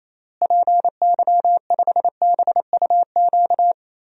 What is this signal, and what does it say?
Morse code, heard as a keyed tone, about 28 wpm, 700 Hz. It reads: PY5BUQ
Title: DXCC-100 Call Signs - 6 Characters - Repeated 3x - 28wpm